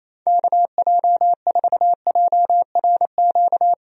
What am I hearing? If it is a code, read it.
Morse code, heard as a keyed tone, about 28 wpm, 700 Hz. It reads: KJ4JRQ
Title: USA Call Signs - 6 Characters - Repeated 2x - 28wpm